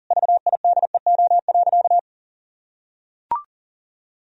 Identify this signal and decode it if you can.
Morse code, heard as a keyed tone, about 40 wpm, 700 Hz. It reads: VIDEO.
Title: Sentences from Top 1000 Words - Review Part 4 - 40wpm